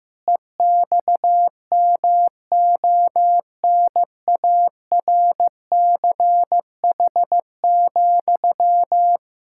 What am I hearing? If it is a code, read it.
Morse code, heard as a keyed tone, about 15 wpm, 700 Hz. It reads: EXMONARCH,
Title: Pangram Challenge v2 - 15wpm